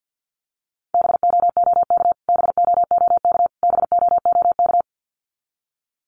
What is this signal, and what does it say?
Morse code, heard as a keyed tone, about 50 wpm, 700 Hz. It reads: BOOK BOOK BOOK
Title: Top 300 Words Repeated 50wpm